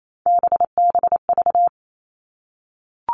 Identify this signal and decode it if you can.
Morse code, heard as a keyed tone, about 28 wpm, 700 Hz. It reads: BBV
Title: All Cut Numbers - 3-Digit 28wpm